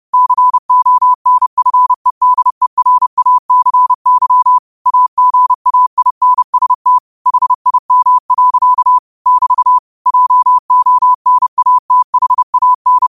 Morse code, heard as a keyed tone, about 30 wpm, 1000 Hz. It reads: CONFEDERACY AGAINST HIM. <BT> JONATHAN